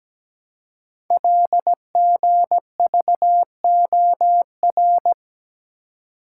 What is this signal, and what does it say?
Morse code, heard as a keyed tone, about 17 wpm, 700 Hz. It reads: LGVOR
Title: Code Groups - 5 Letters - 17wpm